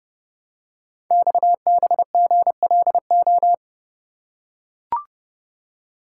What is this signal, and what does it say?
Morse code, heard as a keyed tone, about 30 wpm, 700 Hz. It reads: XBGLO
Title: Code Groups - 5 Letter:Numbers - 30wpm